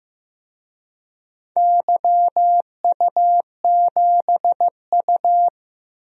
Morse code, heard as a keyed tone, about 15 wpm, 700 Hz. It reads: YU7U
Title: DXCC-100 Call Signs - 4 Characters - Repeated 3x - 15wpm